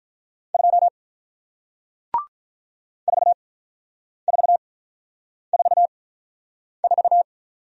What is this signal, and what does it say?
Morse code, heard as a keyed tone, about 48 wpm, spaced as for 9 wpm, 700 Hz. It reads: 2 4444